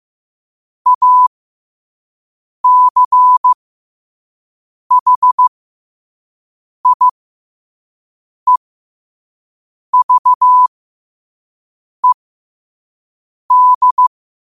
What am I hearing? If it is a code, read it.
Morse code, heard as a keyed tone, about 15 wpm, spaced as for 5 wpm, 1000 Hz. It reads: ACHIEVED
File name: qotdfiles / 5wpm